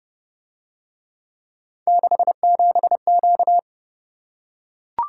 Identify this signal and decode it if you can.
Morse code, heard as a keyed tone, about 30 wpm, 700 Hz. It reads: 67Q